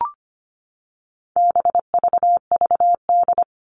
Morse code, heard as a keyed tone, about 25 wpm, 700 Hz. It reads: BVVD
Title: All Cut Numbers - 4-Digit 25wpm